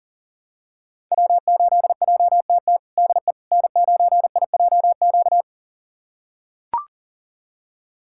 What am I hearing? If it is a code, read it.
Morse code, heard as a keyed tone, about 40 wpm, 700 Hz. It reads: W8JTT DE N9IJQ
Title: USA Callsign DE Callsign - 5 Characters - 40wpm